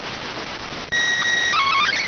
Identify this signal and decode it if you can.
SSTV (slow-scan television), PD 290.